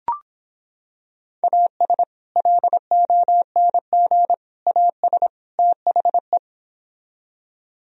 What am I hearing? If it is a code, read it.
Morse code, heard as a keyed tone, about 26 wpm, 700 Hz. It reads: AS LONG AS THE